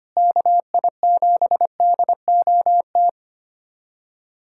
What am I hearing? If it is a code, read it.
Morse code, heard as a keyed tone, about 25 wpm, 700 Hz. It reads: KI7DOT
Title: USA Callsign DE Callsign - 6 Characters - 25wpm